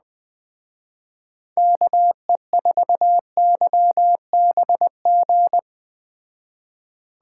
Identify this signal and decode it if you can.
Morse code, heard as a keyed tone, about 20 wpm, 700 Hz. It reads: KE4YBG
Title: Speed Racing - USA 2x3 Call Signs 17wpm